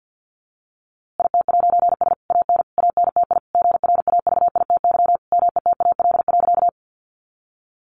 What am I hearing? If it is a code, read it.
Morse code, heard as a keyed tone, about 49 wpm, 700 Hz. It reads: IT'S AN ANTI GRAVITY METAL.
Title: A Sound Of Thunder 50wpm